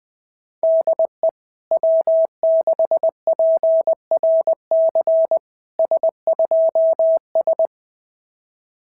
Morse code, heard as a keyed tone, about 20 wpm, 650 Hz. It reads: DE W6PRC S2S
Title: SOTA QSO Practice 20wpm